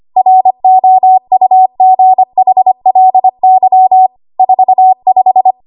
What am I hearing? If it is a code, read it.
Morse code, heard as a keyed tone, about 25 wpm, 750 Hz. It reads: ROUGHLY 45
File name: Archive / 25 WPM / 25WPM